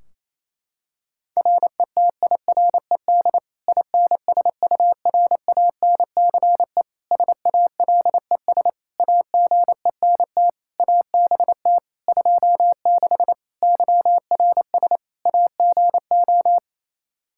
Morse code, heard as a keyed tone, about 28 wpm, 700 Hz. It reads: RETIRED INSURANCE SALES AGENT ABT 26 YRS AGO
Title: Practice QSO - Exchange 3 - More Info - 28wpm